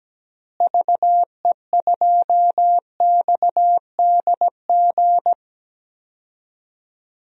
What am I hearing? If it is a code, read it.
Morse code, heard as a keyed tone, about 17 wpm, 700 Hz. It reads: VE2XDG
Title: DXCC-100 Call Signs - 6 Characters - Repeated 3x - 17wpm